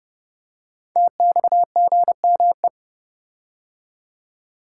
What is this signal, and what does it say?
Morse code, heard as a keyed tone, about 31 wpm, 700 Hz. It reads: TXGME
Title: Code Groups - 5 Letter:Numbers - 30wpm